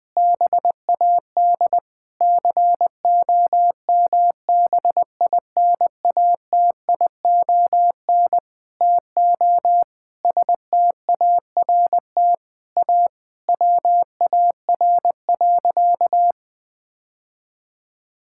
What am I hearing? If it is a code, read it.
Morse code, heard as a keyed tone, about 21 wpm, 700 Hz. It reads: BAD COMBINATION TO START A WAR.
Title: Sentences from Top 1000 Words - Review Part 6 - 20wpm